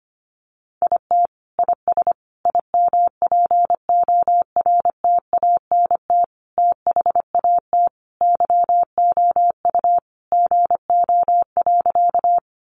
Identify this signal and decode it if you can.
Morse code, heard as a keyed tone, about 25 wpm, 700 Hz. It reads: IT IS IMPORTANT THAT YOU GO.
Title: Sentences from Top 600 Words 25wpm